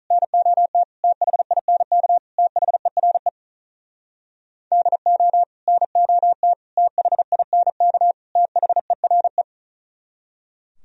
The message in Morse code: NOT THINK THERE DO NOT THINK THERE

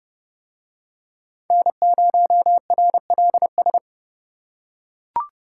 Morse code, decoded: N0RLS

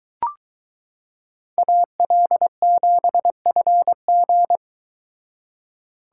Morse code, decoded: AL7FG

